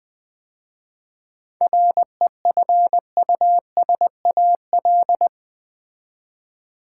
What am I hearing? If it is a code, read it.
Morse code, heard as a keyed tone, about 20 wpm, 700 Hz. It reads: REFUSAL
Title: 7 Letter Words - First 3 Letters Given - 20wpm